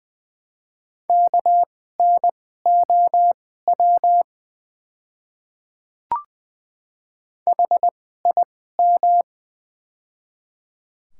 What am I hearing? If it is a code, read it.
Morse code, heard as a keyed tone, about 20 wpm, spaced as for 14 wpm, 700 Hz. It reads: KNOW HIM
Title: 2x Character Spacing - Top 300 Words - 20wpm